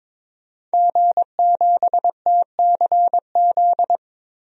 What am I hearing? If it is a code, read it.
Morse code, heard as a keyed tone, about 22 wpm, 700 Hz. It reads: G7TCZ
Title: DXCC-100 Call Signs - 5 Characters - 22wpm